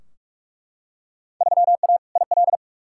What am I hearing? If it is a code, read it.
Morse code, heard as a keyed tone, about 42 wpm, 700 Hz. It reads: 3A IL
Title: Field Day Exchanges - 45wpm